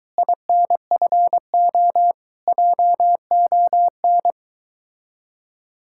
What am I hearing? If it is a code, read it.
Morse code, heard as a keyed tone, about 23 wpm, 700 Hz. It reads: INFO JON